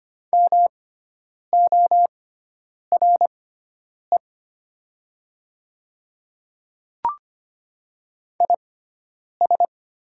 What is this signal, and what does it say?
Morse code, heard as a keyed tone, about 25 wpm, spaced as for 9 wpm, 700 Hz. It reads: MORE IS
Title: 6x Character Spacing - Top 100 Words - 25wpm